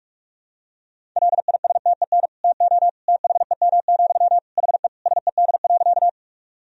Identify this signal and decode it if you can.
Morse code, heard as a keyed tone, about 46 wpm, 700 Hz. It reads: LISTEN TO THEM, HE SED.
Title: Sentences from Top 400 Words and Abbreviations 45wpm